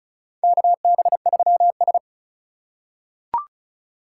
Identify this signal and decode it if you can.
Morse code, heard as a keyed tone, about 36 wpm, 700 Hz. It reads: KB3S
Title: USA Call Signs - 4 Characters - 6x Character Spacing - 35wpm